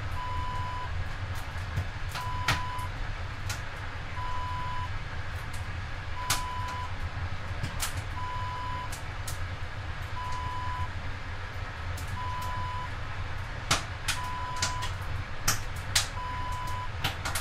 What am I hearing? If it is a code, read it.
There is no code.